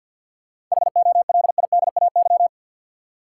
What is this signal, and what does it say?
Morse code, heard as a keyed tone, about 50 wpm, 700 Hz. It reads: HOLIDAY
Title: Seven Letter Words - 50wpm